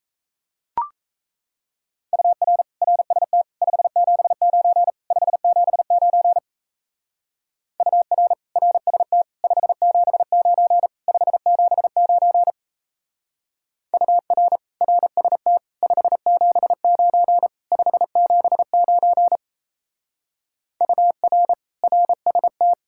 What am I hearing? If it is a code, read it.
Morse code, heard as a keyed tone, about 35 wpm, 700 Hz. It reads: UR RST 579 579 UR RST 579 579 UR RST 579 579 UR RST
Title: Speed-Racing Common QSO Phrases - 28wpm